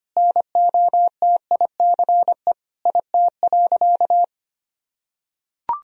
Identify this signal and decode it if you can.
Morse code, heard as a keyed tone, about 25 wpm, 700 Hz. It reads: NOTICE IT.
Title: Sentences from Top 1000 Words - Review Part 2 - 25wpm